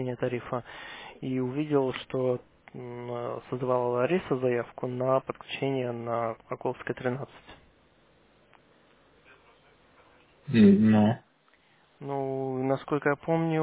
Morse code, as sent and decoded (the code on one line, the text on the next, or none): none